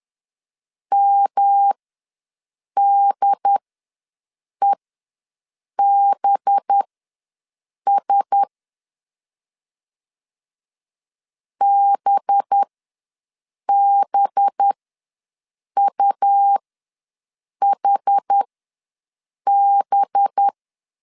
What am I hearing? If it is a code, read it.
Morse code, heard as a keyed tone, about 10 wpm, spaced as for 6 wpm, 800 Hz. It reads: MDEBS BBUHB
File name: LEZIONI / 5 LBVC / 5 LB ws 10wpm 3616